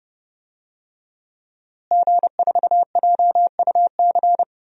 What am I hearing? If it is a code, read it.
Morse code, heard as a keyed tone, about 32 wpm, 700 Hz. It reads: G4JUC
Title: DXCC-100 Call Signs - 5 Characters - Repeated 3x - 30wpm